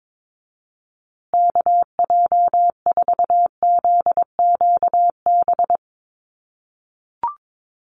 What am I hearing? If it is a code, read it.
Morse code, heard as a keyed tone, about 22 wpm, 700 Hz. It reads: KJ4ZQB
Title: USA Call Signs Repeated 2x 22wpm